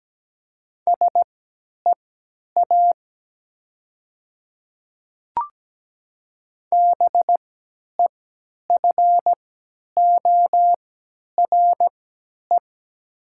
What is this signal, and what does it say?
Morse code, heard as a keyed tone, about 17 wpm, spaced as for 10 wpm, 700 Hz. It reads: SEA BEFORE
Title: 3x Character Spacing - Top 500 Words - 17wpm